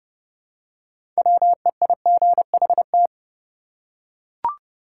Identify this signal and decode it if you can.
Morse code, heard as a keyed tone, about 30 wpm, 700 Hz. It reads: WEIGHT
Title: Six Letter Words - 30wpm